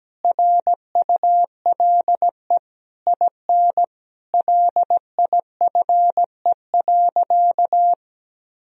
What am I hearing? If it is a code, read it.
Morse code, heard as a keyed tone, about 17 wpm, 700 Hz. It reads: RULE IN LIFE.
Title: Sentences from Top 1000 Words - Review Part 2 - 17wpm